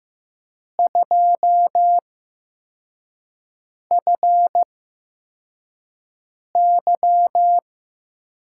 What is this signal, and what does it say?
Morse code, heard as a keyed tone, about 15 wpm, spaced as for 4 wpm, 700 Hz. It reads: 2FY
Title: USA Call Signs - 6 Characters - 8x Character Spacing - 15wpm